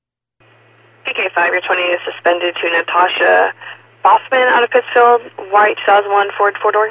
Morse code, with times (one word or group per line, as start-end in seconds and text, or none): none